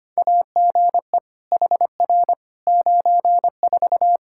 Morse code, heard as a keyed tone, about 25 wpm, 700 Hz. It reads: AGE HR 94
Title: Practice QSO - Exchange 3 - More Info - 25wpm